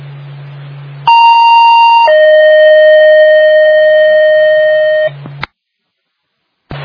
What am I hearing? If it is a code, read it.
Two-tone page, heard as a one-second tone, then a three-second tone, about 930 then 620 Hz. It is heard from 1.1 to 5.1 s.